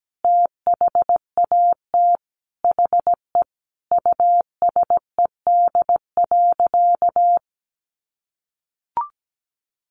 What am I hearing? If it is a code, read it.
Morse code, heard as a keyed tone, about 17 wpm, 700 Hz. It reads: THAT HE USED.
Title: Sentences from Top 1500 Words 17wpm